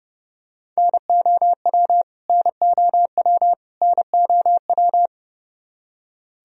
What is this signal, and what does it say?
Morse code, heard as a keyed tone, about 30 wpm, 700 Hz. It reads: NOW NOW NOW
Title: Top 100 Words Repeated 30wpm